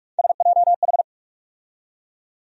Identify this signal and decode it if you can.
Morse code, heard as a keyed tone, about 42 wpm, 700 Hz. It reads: SJH
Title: USA Call Sign Postfixes - 3 Letters 45wpm